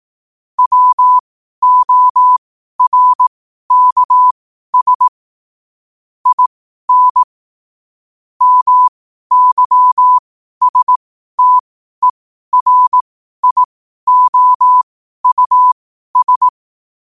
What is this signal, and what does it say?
Morse code, heard as a keyed tone, about 18 wpm, spaced as for 12 wpm, 1000 Hz. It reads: WORKS IN MYSTERIOUS
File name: qotdfiles / 12wpm